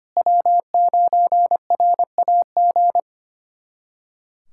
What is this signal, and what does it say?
Morse code, heard as a keyed tone, about 25 wpm, 700 Hz. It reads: W9RAG
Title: USA Callsign DE Callsign - 5 Characters - 25wpm